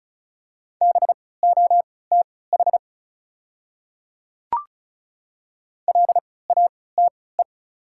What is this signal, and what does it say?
Morse code, heard as a keyed tone, about 36 wpm, spaced as for 20 wpm, 700 Hz. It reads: BOTH LATE
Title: 3x Character Spacing - Top 500 Words - 35wpm